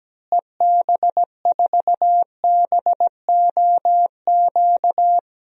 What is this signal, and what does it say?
Morse code, heard as a keyed tone, about 17 wpm, 700 Hz. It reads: EB4BOQ
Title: DXCC-100 Call Signs - 6 Characters - Repeated 3x - 17wpm